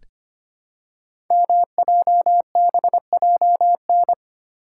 Morse code, heard as a keyed tone, about 25 wpm, 700 Hz. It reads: MJBJN
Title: Code Groups - 5 Letters - 25wpm